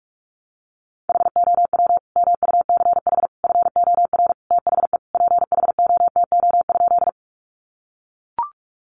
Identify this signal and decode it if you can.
Morse code, heard as a keyed tone, about 44 wpm, 700 Hz. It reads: HOW MUCH FOR THE PHOTO?
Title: Sentences from Top 1200 Words 45wpm